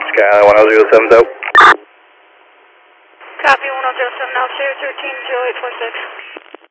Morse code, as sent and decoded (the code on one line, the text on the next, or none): none